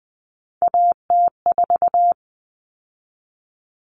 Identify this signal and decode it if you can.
Morse code, heard as a keyed tone, about 20 wpm, 700 Hz. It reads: AT4
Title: Common Cut Numbers - 3-Digit 20wpm